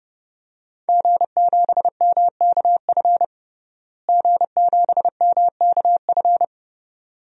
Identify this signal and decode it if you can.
Morse code, heard as a keyed tone, about 30 wpm, 700 Hz. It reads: G7MKF G7MKF
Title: DXCC-100 Call Signs - 5 Characters - Repeated 2x - 30wpm